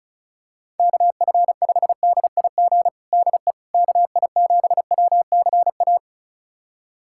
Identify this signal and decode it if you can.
Morse code, heard as a keyed tone, about 36 wpm, 690 Hz. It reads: KF5DIG DE KI7WCA
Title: USA Callsign DE Callsign 35wpm